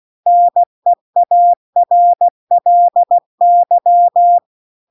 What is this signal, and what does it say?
Morse code, heard as a keyed tone, about 16 wpm, 700 Hz. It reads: NEARLY